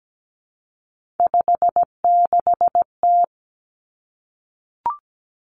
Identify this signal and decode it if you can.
Morse code, heard as a keyed tone, about 17 wpm, 700 Hz. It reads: 56T
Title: Three Challenging Letter-Number Combinations 17wpm